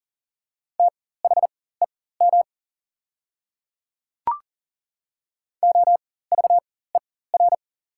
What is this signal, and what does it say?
Morse code, heard as a keyed tone, about 40 wpm, spaced as for 19 wpm, 690 Hz. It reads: THEM OVER
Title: 4x Character Spacing - Top 100 Words - 40wpm